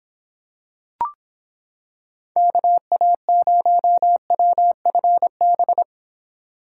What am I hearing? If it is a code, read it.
Morse code, heard as a keyed tone, about 26 wpm, 700 Hz. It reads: KA0WFB